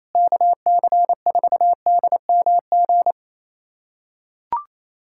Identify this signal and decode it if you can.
Morse code, heard as a keyed tone, about 28 wpm, 700 Hz. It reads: KC4DMG
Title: USA Callsign DE Callsign - 6 Characters - 28wpm